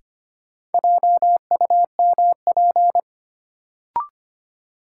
Morse code, heard as a keyed tone, about 25 wpm, 700 Hz. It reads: JUMP
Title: Sentences from Top 1300 Words - New Word Drill - 25wpm